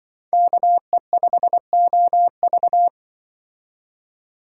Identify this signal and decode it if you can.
Morse code, heard as a keyed tone, about 24 wpm, 700 Hz. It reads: KE5OV
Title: Speed Racing - USA 2x2 Call Signs 20wpm